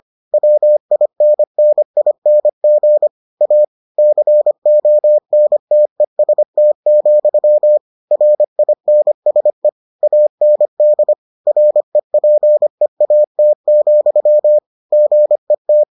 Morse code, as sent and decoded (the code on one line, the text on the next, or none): .-- .. -. -. .. -. --. / .- / -.-. --- -. - . ... - --..-- / .-. .. -. ... / .- -. -.. / .-. .--. .- - --..-- / --. -
WINNING A CONTEST, RINSE AND REPEAT, GET